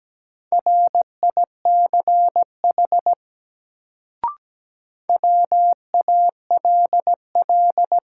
Morse code, heard as a keyed tone, about 17 wpm, 700 Hz. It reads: RICH WALL
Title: Sentences from Top 1100 Words - New Word Drill Alt - 17wpm